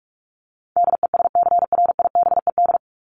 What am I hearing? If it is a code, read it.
Morse code, heard as a keyed tone, about 44 wpm, 700 Hz. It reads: DESCRIBED
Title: ED Suffix - Top 100 Words with Base Words 45wpm